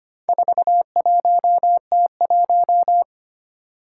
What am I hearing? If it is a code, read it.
Morse code, heard as a keyed tone, about 25 wpm, 700 Hz. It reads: 41T1
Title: Four Challenging Letter-Number Combinations 25wpm